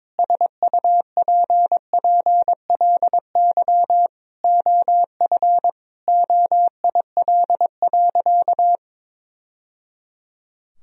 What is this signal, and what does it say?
Morse code, heard as a keyed tone, about 22 wpm, 700 Hz. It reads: SUPPLY OF OIL.
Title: Sentences from Top 1000 Words - Review Part 4 - 22wpm